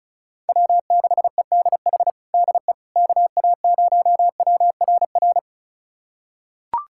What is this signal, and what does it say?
Morse code, heard as a keyed tone, about 36 wpm, 700 Hz. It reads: W6EDH DE KA0WRR
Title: USA Callsign DE Callsign 35wpm